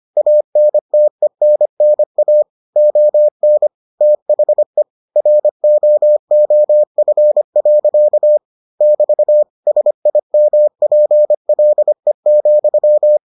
Morse code, heard as a keyed tone, about 25 wpm, 600 Hz. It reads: ANTENNA ON THE ROOF. <BT> SIMPLE,